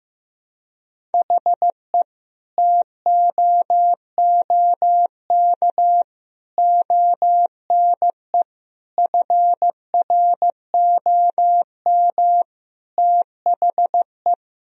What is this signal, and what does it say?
Morse code, heard as a keyed tone, about 16 wpm, 700 Hz. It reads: HE TOOK ONE FROM THE
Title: Sentences from Top 1000 Words - Review Part 1 - 15wpm